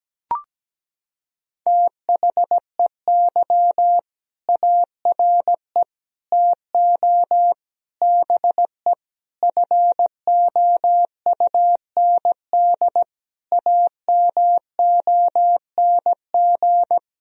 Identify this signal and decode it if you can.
Morse code, heard as a keyed tone, about 17 wpm, 700 Hz. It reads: THEY ARE TO BE FOUND AMONG
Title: Sentences from Top 1000 Words - Review Part 3 - 17wpm